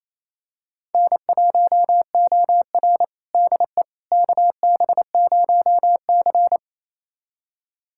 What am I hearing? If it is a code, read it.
Morse code, heard as a keyed tone, about 28 wpm, 700 Hz. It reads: N1OR DE KB0C
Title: USA Callsign DE Callsign - 4 Characters - 28wpm